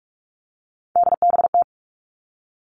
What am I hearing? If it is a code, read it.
Morse code, heard as a keyed tone, about 46 wpm, 700 Hz. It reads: DBT
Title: Three Challenging Letter-Number Combinations 45wpm